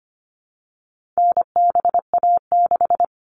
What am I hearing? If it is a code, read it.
Morse code, heard as a keyed tone, about 25 wpm, 700 Hz. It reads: NBA6
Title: All Cut Numbers - 4-Digit 25wpm